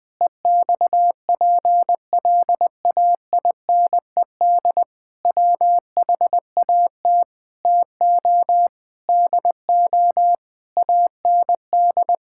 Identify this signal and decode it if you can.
Morse code, heard as a keyed tone, about 20 wpm, 700 Hz. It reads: EXPLAINED WHAT TO DO AND
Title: Sentences with 8-Word Phrases - Spoken Beginning - 20wpm